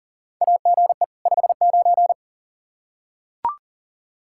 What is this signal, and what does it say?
Morse code, heard as a keyed tone, about 40 wpm, 700 Hz. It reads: AGE 59
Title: Practice QSO - Exchange 3 - More Info - 40wpm